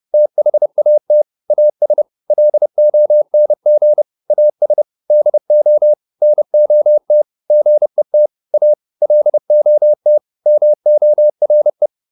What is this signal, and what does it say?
Morse code, heard as a keyed tone, about 30 wpm, 600 Hz. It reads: THAT AS LONG AS DO NOT GET A LOT MORE